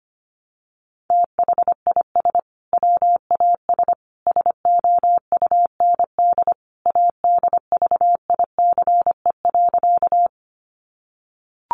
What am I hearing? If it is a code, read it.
Morse code, heard as a keyed tone, about 25 wpm, 700 Hz. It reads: THIS WAS SOUND ADVICE.
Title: Sentences from Top 1100 Words 25wpm